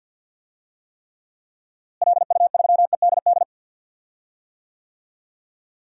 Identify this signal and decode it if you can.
Morse code, heard as a keyed tone, about 47 wpm, 700 Hz. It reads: LU3EDD
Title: DXCC-100 Call Signs - 6 Characters - Repeated 2x - 50wpm